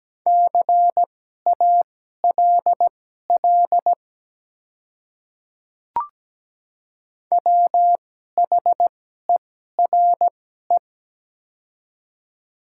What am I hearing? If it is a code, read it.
Morse code, heard as a keyed tone, about 17 wpm, spaced as for 12 wpm, 700 Hz. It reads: CALL WHERE
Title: 2x Character Spacing - Top 200 Words - 17wpm